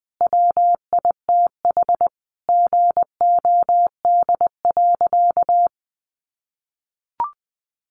Morse code, heard as a keyed tone, about 20 wpm, 700 Hz. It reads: WITH GOD.
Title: A Sound Of Thunder 20wpm